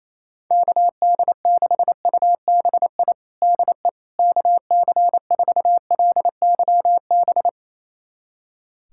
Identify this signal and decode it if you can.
Morse code, heard as a keyed tone, about 28 wpm, 700 Hz. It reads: KD6UBI DE KC4LYB
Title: USA Callsign DE Callsign - 6 Characters - 28wpm